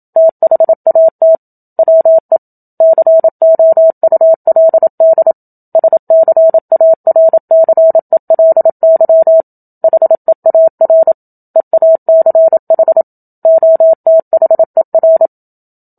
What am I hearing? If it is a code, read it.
Morse code, heard as a keyed tone, about 27 wpm, 650 Hz. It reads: THAT WE COULD SCARCELY HEAR EACH OTHER
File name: Book Two / 01-Chapter 1 - 27 WPM